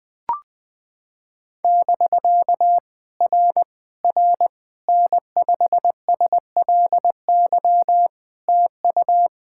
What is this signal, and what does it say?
Morse code, heard as a keyed tone, about 19 wpm, 700 Hz. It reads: <BK> R R N5SLY TU
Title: POTA QSO Practice - 20wpm